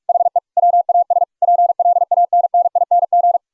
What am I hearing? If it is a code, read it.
Morse code, heard as a keyed tone, about 44 wpm, 700 Hz. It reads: HE WAS PLANNING